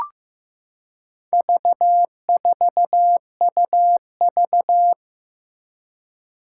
Morse code, heard as a keyed tone, about 15 wpm, 700 Hz. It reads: V4UV
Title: Four Challenging Letter-Number Combinations 15wpm